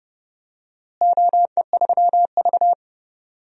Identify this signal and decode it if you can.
Morse code, heard as a keyed tone, about 30 wpm, 700 Hz. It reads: OE3V
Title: DXCC-100 Call Signs - 4 Characters - Repeated 3x - 30wpm